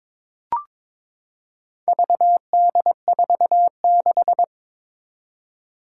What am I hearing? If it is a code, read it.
Morse code, heard as a keyed tone, about 22 wpm, 700 Hz. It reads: VD46
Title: Four Challenging Letter-Number Combinations 22wpm